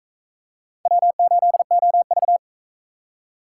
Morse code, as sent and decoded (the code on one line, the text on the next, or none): .-- ---.. --- ...-
W8OV